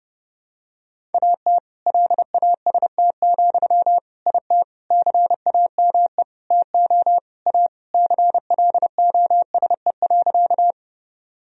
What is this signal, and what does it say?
Morse code, heard as a keyed tone, about 30 wpm, 700 Hz. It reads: AT LAST, IT CAME TO A CLOSE.
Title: Sentences from Top 1000 Words - Review Part 1 - 30wpm